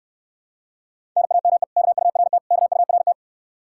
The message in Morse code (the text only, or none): NAME KURT KURT